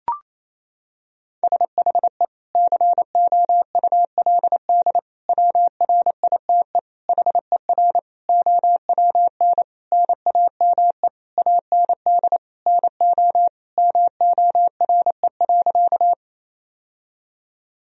Morse code, heard as a keyed tone, about 28 wpm, 700 Hz. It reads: SHE COULD WRITE HER OWN NAME AND NO MORE.
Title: Sentences from Top 1000 Words - Review Part 1 - 28wpm